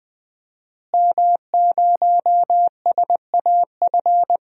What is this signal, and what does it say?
Morse code, heard as a keyed tone, about 20 wpm, 700 Hz. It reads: M0SAF
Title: DXCC-100 Call Signs - 5 Characters - Repeated 2x - 20wpm